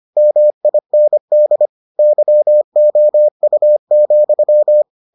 Morse code, heard as a keyed tone, about 25 wpm, 600 Hz. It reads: MIND YOU,